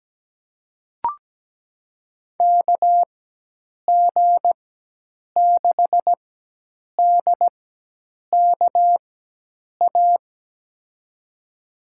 Morse code, heard as a keyed tone, about 17 wpm, spaced as for 8 wpm, 700 Hz. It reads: KG6DKA
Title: USA Call Signs - 6 Characters - 4x Character Spacing - 17wpm